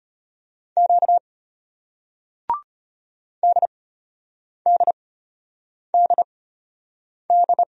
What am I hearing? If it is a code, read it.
Morse code, heard as a keyed tone, about 34 wpm, spaced as for 8 wpm, 700 Hz. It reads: Q DDDD